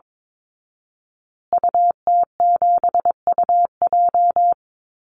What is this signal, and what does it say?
Morse code, heard as a keyed tone, about 22 wpm, 700 Hz. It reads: UT7UJ
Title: Call Signs Repeated 2x - 22wpm